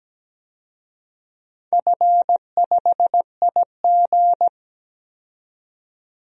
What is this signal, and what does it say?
Morse code, heard as a keyed tone, about 17 wpm, 700 Hz. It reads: F5IG
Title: DXCC-100 Call Signs - 4 Characters - 17wpm